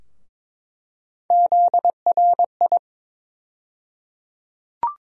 Morse code, heard as a keyed tone, about 22 wpm, 700 Hz. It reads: ZRI
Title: USA Call Sign Postfixes - 3 Letters 22wpm